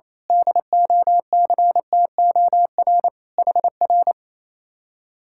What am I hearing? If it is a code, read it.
Morse code, heard as a keyed tone, about 26 wpm, 700 Hz. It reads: DOCTOR HR
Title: Speed-Racing Common QSO Phrases Encore - 28wpm